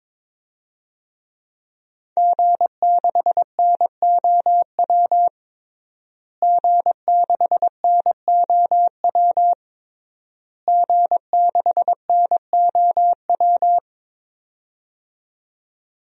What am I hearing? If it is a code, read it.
Morse code, heard as a keyed tone, about 22 wpm, 700 Hz. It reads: G6NOW G6NOW G6NOW
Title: DXCC-100 Call Signs - 5 Characters - Repeated 3x - 22wpm